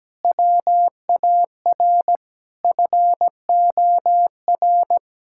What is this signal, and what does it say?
Morse code, heard as a keyed tone, about 17 wpm, 700 Hz. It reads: WAR FOR
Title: Sentences from Top 1000 Words - Review Part 6 - 17wpm